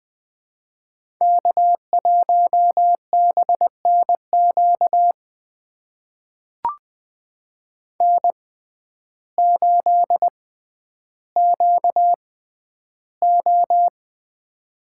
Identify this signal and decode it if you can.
Morse code, heard as a keyed tone, about 20 wpm, spaced as for 11 wpm, 700 Hz. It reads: K1BNQ N8QO
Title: USA Call Signs - 6x Character Spacing - 20wpm